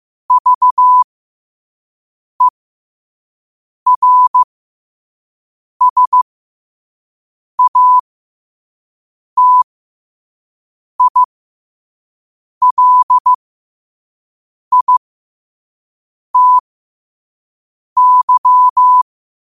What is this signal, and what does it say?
Morse code, heard as a keyed tone, about 15 wpm, spaced as for 5 wpm, 1000 Hz. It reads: VERSATILITY